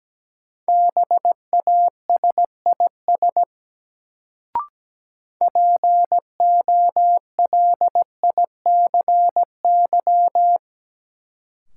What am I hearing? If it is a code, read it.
Morse code, heard as a keyed tone, about 17 wpm, 700 Hz. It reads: BASIS POLICY
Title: Sentences from Top 1200 Words - New Word Drill Alt - 17wpm